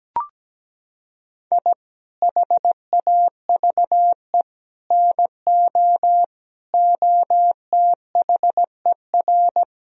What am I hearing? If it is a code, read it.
Morse code, heard as a keyed tone, about 17 wpm, 700 Hz. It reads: I HAVE NO OTHER
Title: Sentences from Top 1000 Words - Review Part 3 - 17wpm